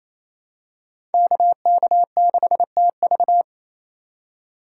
Morse code, heard as a keyed tone, about 28 wpm, 700 Hz. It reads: KK6TV
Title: K1USN SST QSO Practice - 28wpm